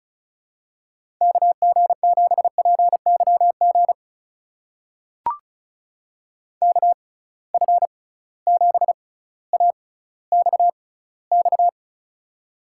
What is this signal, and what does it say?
Morse code, heard as a keyed tone, about 34 wpm, spaced as for 18 wpm, 700 Hz. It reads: KG7PYG KF7AXX
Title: USA Call Signs - 6 Characters - 6x Character Spacing - 35wpm